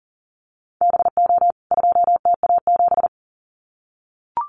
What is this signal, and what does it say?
Morse code, heard as a keyed tone, about 40 wpm, 700 Hz. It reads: BO 2TA7
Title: CWT Exchange 40wpm